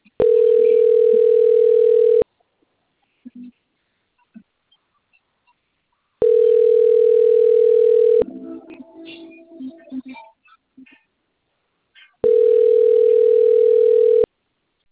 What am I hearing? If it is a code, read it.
Ringback tone (North America), 440 + 480 Hz.